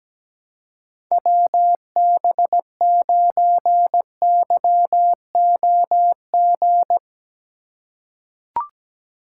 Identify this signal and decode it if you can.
Morse code, heard as a keyed tone, about 17 wpm, 700 Hz. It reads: WB9YOG
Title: USA Call Signs - 6 Characters - 4x Character Spacing - 17wpm